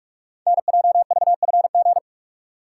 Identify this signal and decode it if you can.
Morse code, heard as a keyed tone, about 44 wpm, 690 Hz. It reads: NJVFG
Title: Code Groups - 5 Letters - 45wpm